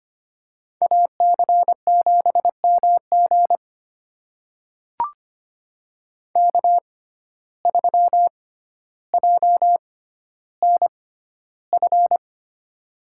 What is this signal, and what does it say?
Morse code, heard as a keyed tone, about 25 wpm, spaced as for 13 wpm, 700 Hz. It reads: AC7MG K3JNF
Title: USA Call Signs - 5 Characters - 6x Character Spacing - 25wpm